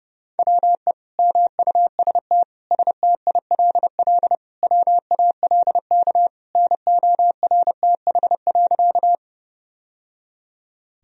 Morse code, heard as a keyed tone, about 30 wpm, 700 Hz. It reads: WE MUST STILL WALK NORTH.
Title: Sentences from Top 1000 Words - Review Part 1 - 30wpm